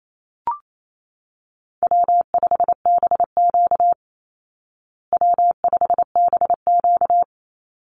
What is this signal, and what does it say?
Morse code, heard as a keyed tone, about 28 wpm, 700 Hz. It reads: W5BQ W5BQ
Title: Call Signs Repeated 3x - 28wpm